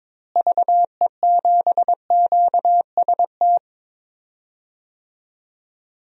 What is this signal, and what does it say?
Morse code, heard as a keyed tone, about 22 wpm, 700 Hz. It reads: VE7QST
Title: DXCC-100 Call Signs - 6 Characters - Repeated 3x - 22wpm